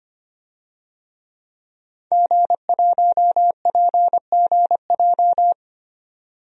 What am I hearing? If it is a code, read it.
Morse code, heard as a keyed tone, about 25 wpm, 700 Hz. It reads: G1PGJ
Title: DXCC-100 Call Signs - 5 Characters - 25wpm